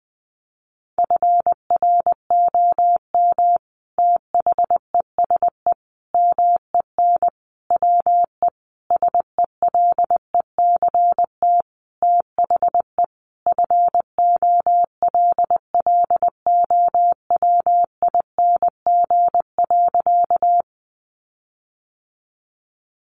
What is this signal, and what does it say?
Morse code, heard as a keyed tone, about 19 wpm, 700 Hz. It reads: FROM THESE MEN WE SELECT THE FOLLOWING.
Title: Sentences from Top 1300 Words 20wpm